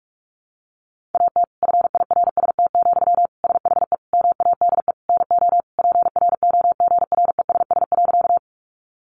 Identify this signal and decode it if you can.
Morse code, heard as a keyed tone, about 44 wpm, 700 Hz. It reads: AT FIRST, SHE MADE NO PROGRESS.
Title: Sentences from Top 1400 Words 45wpm